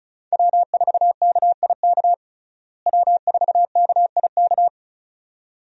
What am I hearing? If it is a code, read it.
Morse code, heard as a keyed tone, about 36 wpm, 700 Hz. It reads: W4KIK W4KIK
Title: USA Call Signs - 5 Characters - Repeated 2x - 35wpm